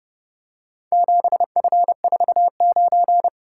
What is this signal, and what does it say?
Morse code, heard as a keyed tone, about 28 wpm, 700 Hz. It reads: 7F49